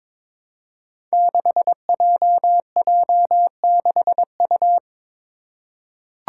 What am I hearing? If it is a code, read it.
Morse code, heard as a keyed tone, about 22 wpm, 700 Hz. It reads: 6JJ6U
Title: Five Challenging Letter-Number Combinations 22wpm